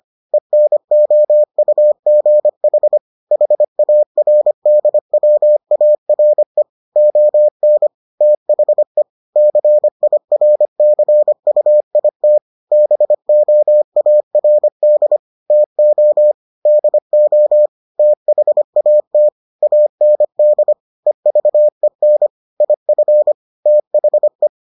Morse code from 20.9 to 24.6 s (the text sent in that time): EVEN IF THE